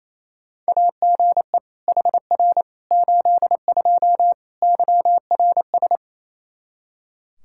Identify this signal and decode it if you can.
Morse code, heard as a keyed tone, about 28 wpm, 700 Hz. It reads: AGE HR 82 YRS
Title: K1USN SST QSO Practice 28wpm